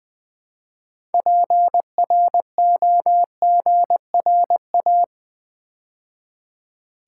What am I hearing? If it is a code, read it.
Morse code, heard as a keyed tone, about 20 wpm, 700 Hz. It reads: PROGRA
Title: Build A Word - 3 Letters At A Time - 9-Letter Words - 20wpm